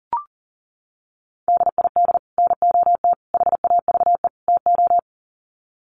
Morse code, heard as a keyed tone, about 40 wpm, 700 Hz. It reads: DID NOT HAVE TO
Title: Sets of 4 Words 40wpm